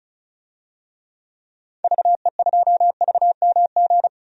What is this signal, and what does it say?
Morse code, heard as a keyed tone, about 36 wpm, 700 Hz. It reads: VE2VMG
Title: DXCC-100 Call Signs - 6 Characters - Repeated 2x - 35wpm